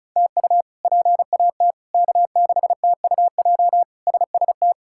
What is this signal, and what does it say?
Morse code, heard as a keyed tone, about 36 wpm, 700 Hz. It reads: TU PAT K6TUJ SST
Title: K1USN SST QSO Practice - 35wpm